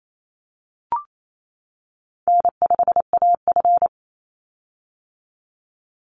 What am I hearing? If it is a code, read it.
Morse code, heard as a keyed tone, about 28 wpm, 700 Hz. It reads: N5AF
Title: Easy Call Signs - 28wpm